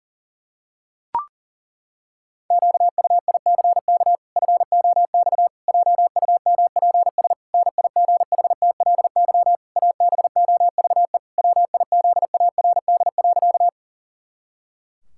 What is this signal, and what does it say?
Morse code, heard as a keyed tone, about 40 wpm, 700 Hz. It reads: QUICK FOX JUMPS NIGHTLY ABOVE WIZARD.
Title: Pangram Challenge v2 - 40wpm